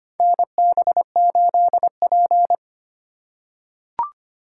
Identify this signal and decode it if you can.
Morse code, heard as a keyed tone, about 25 wpm, 700 Hz. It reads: NB8P
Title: USA Call Signs - 2x Character Spacing - 25wpm